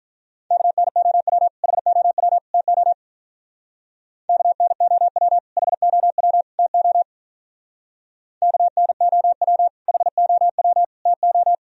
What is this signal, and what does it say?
Morse code, heard as a keyed tone, about 45 wpm, 700 Hz. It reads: KNOW HOW TO KNOW HOW TO KNOW HOW TO